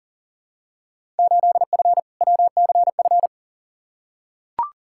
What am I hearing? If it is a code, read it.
Morse code, heard as a keyed tone, about 40 wpm, 700 Hz. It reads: 8F WCF
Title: Field Day Exchanges Encore - 40wpm